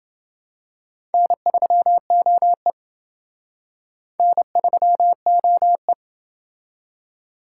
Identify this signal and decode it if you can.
Morse code, heard as a keyed tone, about 28 wpm, 700 Hz. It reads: N3OE N3OE